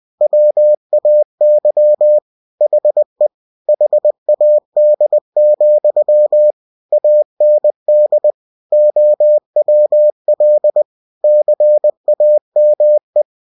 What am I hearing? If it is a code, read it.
Morse code, heard as a keyed tone, about 20 wpm, 600 Hz. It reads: WAY HE HAD, AND OWL CAME
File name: cw-books / winnie-the-pooh / winnie-20wpm-page-0102